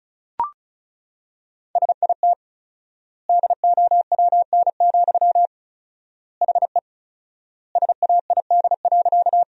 Morse code, heard as a keyed tone, about 36 wpm, 700 Hz. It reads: SIT DOWN, HE SAID.
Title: 4x Word-Spacing - Sentences from Top 500 Words - 35wpm